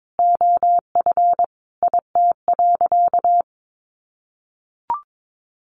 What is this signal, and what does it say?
Morse code, heard as a keyed tone, about 22 wpm, 700 Hz. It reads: OF IT.
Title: Sentences from Top 500 Words 22wpm